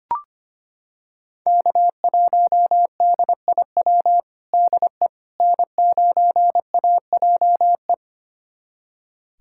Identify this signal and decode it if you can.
Morse code, heard as a keyed tone, about 25 wpm, 700 Hz. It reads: K1DIW DE N9AJE
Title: USA Callsign DE Callsign - 5 Characters - 25wpm